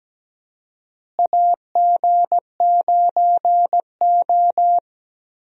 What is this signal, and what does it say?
Morse code, heard as a keyed tone, about 17 wpm, 700 Hz. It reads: AG9O
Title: USA Call Signs - 4 Characters - Repeated 2x - 17wpm